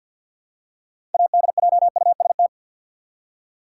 Morse code, heard as a keyed tone, about 53 wpm, 700 Hz. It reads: ADJUST
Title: Six Letter Words - 50wpm